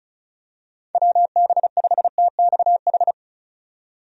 Morse code, heard as a keyed tone, about 34 wpm, 700 Hz. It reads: WB5TXH
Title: USA Call Signs - 6 Characters - 35wpm